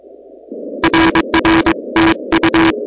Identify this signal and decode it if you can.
Morse code, heard as a keyed tone, about 22 wpm, 330 Hz, with noise. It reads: RR TU
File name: contests / 2019-cw-wpx-kt5j-k5pi- -2330 0-16